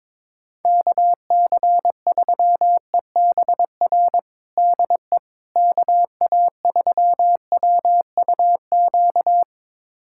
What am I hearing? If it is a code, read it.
Morse code, heard as a keyed tone, about 22 wpm, 700 Hz. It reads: KC3EBR DE KA3WUQ
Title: USA Callsign DE Callsign - 6 Characters - 22wpm